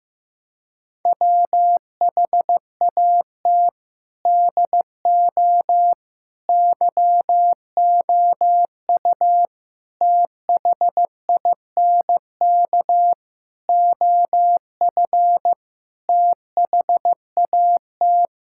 Morse code, heard as a keyed tone, about 15 wpm, 700 Hz. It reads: WHAT DO YOU THINK OF THAT
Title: Sentences from Top 1000 Words - Review Part 6 - 15wpm